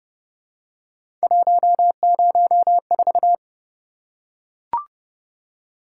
Morse code, heard as a keyed tone, about 26 wpm, 700 Hz. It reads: 104